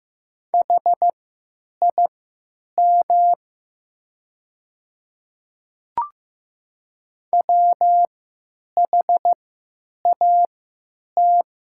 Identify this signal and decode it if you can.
Morse code, heard as a keyed tone, about 15 wpm, spaced as for 9 wpm, 690 Hz. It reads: HIM WHAT
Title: 3x Character Spacing - Top 100 Words - 15wpm